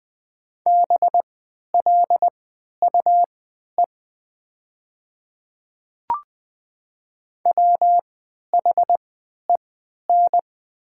Morse code, heard as a keyed tone, about 20 wpm, spaced as for 11 wpm, 700 Hz. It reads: BLUE WHEN
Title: 3x Character Spacing - Top 500 Words - 20wpm